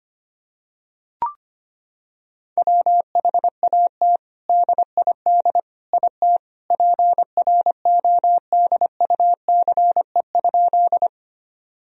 Morse code, heard as a keyed tone, about 25 wpm, 700 Hz. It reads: WHAT DID IT PRODUCE?
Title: Sentences from Top 1000 Words - Review Part 3 - 25wpm